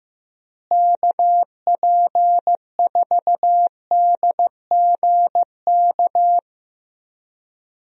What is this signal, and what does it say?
Morse code, heard as a keyed tone, about 15 wpm, 700 Hz. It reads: KP4DGK
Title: USA Call Signs - 6 Characters - Repeated 2x - 15wpm